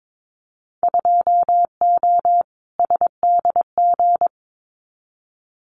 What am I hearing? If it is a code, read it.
Morse code, heard as a keyed tone, about 22 wpm, 700 Hz. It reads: 2O SDG
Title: Winter Field Day Exchanges 22wpm